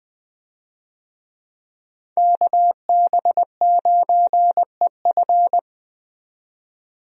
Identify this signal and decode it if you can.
Morse code, heard as a keyed tone, about 20 wpm, 700 Hz. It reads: KB9EF
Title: USA Call Signs - 5 Characters - 20wpm